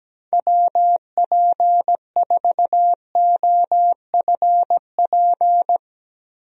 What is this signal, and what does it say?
Morse code, heard as a keyed tone, about 17 wpm, 700 Hz. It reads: WP4OFP
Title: USA Call Signs - 6 Characters - Repeated 2x - 17wpm